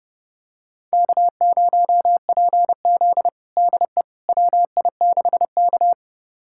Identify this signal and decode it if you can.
Morse code, heard as a keyed tone, about 30 wpm, 700 Hz. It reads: K0PZ DE WI6K
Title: USA Callsign DE Callsign - 4 Characters - 30wpm